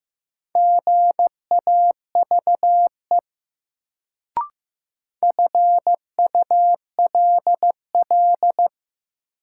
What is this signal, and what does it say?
Morse code, heard as a keyed tone, about 15 wpm, 700 Hz. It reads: GAVE FULL
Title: Sentences from Top 500 Words - New Word Drill - 15wpm